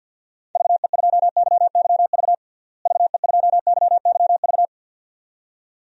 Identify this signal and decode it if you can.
Morse code, heard as a keyed tone, about 53 wpm, 700 Hz. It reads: VE2YYV VE2YYV
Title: DXCC-100 Call Signs - 6 Characters - Repeated 2x - 50wpm